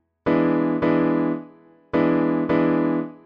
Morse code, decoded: none